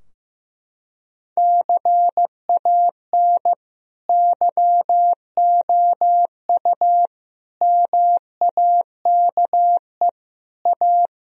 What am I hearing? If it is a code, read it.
Morse code, heard as a keyed tone, about 15 wpm, 700 Hz. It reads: CAN YOU MAKE A
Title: Sentences from Top 1000 Words - Review Part 3 - 15wpm